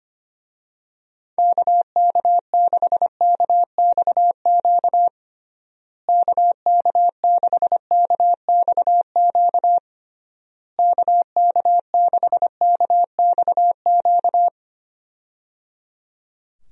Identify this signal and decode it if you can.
Morse code, heard as a keyed tone, about 25 wpm, 700 Hz. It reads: KK6KXQ KK6KXQ KK6KXQ
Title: USA Call Signs - 6 Characters - Repeated 3x - 25wpm